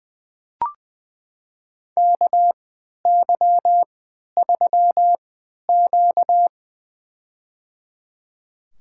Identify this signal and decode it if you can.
Morse code, heard as a keyed tone, about 20 wpm, spaced as for 11 wpm, 700 Hz. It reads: KY3Q
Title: USA Call Signs - 4 Characters - 3x Character Spacing - 20wpm